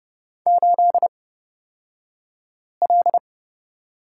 Morse code, heard as a keyed tone, about 30 wpm, spaced as for 5 wpm, 700 Hz. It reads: 8L
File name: Lesson 37d - Letter x - Two Characters 30wpm